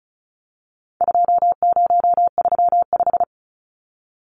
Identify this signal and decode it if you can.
Morse code, heard as a keyed tone, about 34 wpm, 700 Hz. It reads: 2035
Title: Numbers - 4-Digit 35wpm